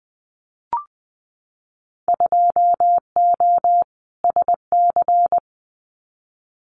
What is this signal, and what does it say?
Morse code, heard as a keyed tone, about 20 wpm, 700 Hz. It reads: 2O SC
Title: Winter Field Day Exchanges Encore 20wpm